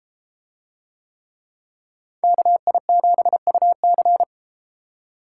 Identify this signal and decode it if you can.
Morse code, heard as a keyed tone, about 30 wpm, 700 Hz. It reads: KI7UC